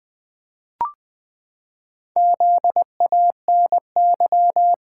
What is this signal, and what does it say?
Morse code, heard as a keyed tone, about 20 wpm, 700 Hz. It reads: ZANY
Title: Pangram Challenge v2 - 20wpm